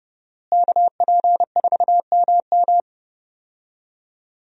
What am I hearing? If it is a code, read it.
Morse code, heard as a keyed tone, about 30 wpm, 700 Hz. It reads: KP4MM
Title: USA Call Signs - 5 Characters - Repeated 2x - 30wpm